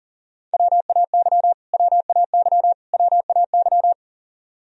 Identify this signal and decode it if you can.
Morse code, heard as a keyed tone, about 40 wpm, 700 Hz. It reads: WAY WAY WAY
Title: Top 100 Words Repeated 40wpm